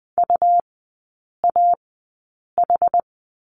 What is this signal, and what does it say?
Morse code, heard as a keyed tone, about 20 wpm, spaced as for 8 wpm, 700 Hz. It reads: UAH
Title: 127 Lesson 21e - Number 6 - Three Characters 20wpm